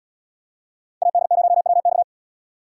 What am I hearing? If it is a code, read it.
Morse code, heard as a keyed tone, about 73 wpm, 700 Hz. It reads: AD0WX